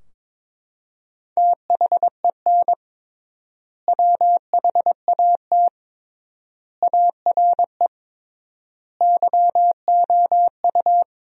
Morse code, heard as a keyed tone, about 22 wpm, 700 Hz. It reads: THEN WHAT ARE YOU
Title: 3x Word-Spacing - Sentences from Top 100 Words - 22wpm